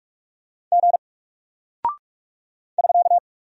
Morse code, heard as a keyed tone, about 44 wpm, 700 Hz. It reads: G <SK>
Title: Single Character (Letter:Number:Punctuation:Prosign) 45wpm